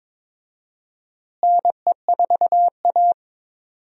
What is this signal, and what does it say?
Morse code, heard as a keyed tone, about 21 wpm, 700 Hz. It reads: NE4A
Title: USA Call Signs - 4 Characters - Repeated 3x - 22wpm